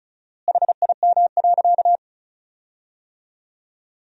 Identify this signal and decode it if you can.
Morse code, heard as a keyed tone, about 34 wpm, 700 Hz. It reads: HIM.
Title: 4x Word-Spacing - Sentences from Top 300 Words - 35wpm